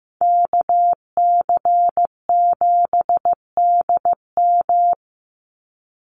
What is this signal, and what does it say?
Morse code, heard as a keyed tone, about 15 wpm, 700 Hz. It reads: KC7DM
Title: NA SOTA Activator Callsigns 15wpm